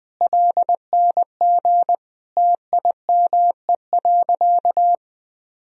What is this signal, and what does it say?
Morse code, heard as a keyed tone, about 20 wpm, 700 Hz. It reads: LNG TIME.
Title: Sentences from Top 400 Words and Abbreviations 20wpm